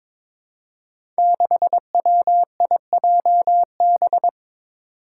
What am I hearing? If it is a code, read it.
Morse code, heard as a keyed tone, about 23 wpm, 700 Hz. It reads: 6WIJB
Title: Five Challenging Letter-Number Combinations 22wpm